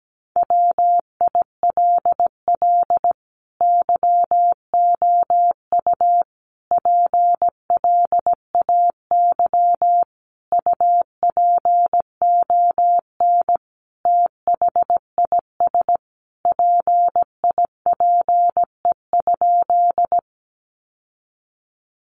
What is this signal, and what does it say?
Morse code, heard as a keyed tone, about 17 wpm, 700 Hz. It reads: WILL YOU PLAY UPON THIS PIPE?
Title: Sentences from Top 1500 Words 17wpm